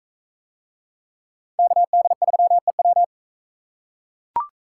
Morse code, heard as a keyed tone, about 42 wpm, 700 Hz. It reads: KD3EW